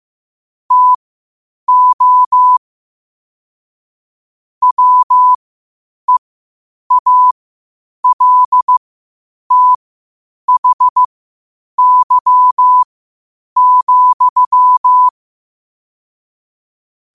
Morse code, heard as a keyed tone, about 15 wpm, spaced as for 8 wpm, 1000 Hz. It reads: TO WEALTHY,
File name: qotdfiles / 8wpm